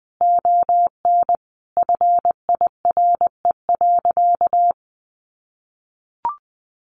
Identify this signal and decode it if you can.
Morse code, heard as a keyed tone, about 20 wpm, 700 Hz. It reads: ON FIRE.
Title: Sentences from Top 400 Words 20wpm